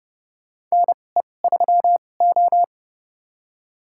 Morse code, heard as a keyed tone, about 30 wpm, spaced as for 22 wpm, 700 Hz. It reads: NE3O
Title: USA Call Signs - 4 Characters - 2x Character Spacing - 30wpm